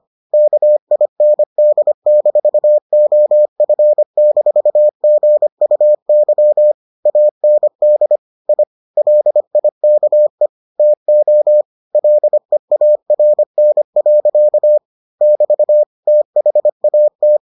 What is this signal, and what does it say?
Morse code, heard as a keyed tone, about 25 wpm, 600 Hz. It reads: KIND-OF-GUY AND I LIKE TO LEARN. <BT> THAT